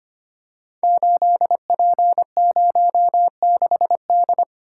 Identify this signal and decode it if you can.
Morse code, heard as a keyed tone, about 25 wpm, 700 Hz. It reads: 8P06D
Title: Code Groups - 5 Letter:Numbers - 25wpm